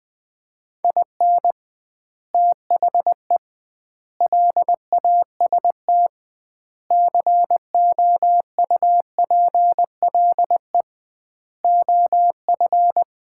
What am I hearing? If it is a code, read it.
Morse code, heard as a keyed tone, about 20 wpm, 700 Hz. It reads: IN THE LAST COUPLE OF
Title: 2x Word-Spacing - Sets of 5 Words - 20wpm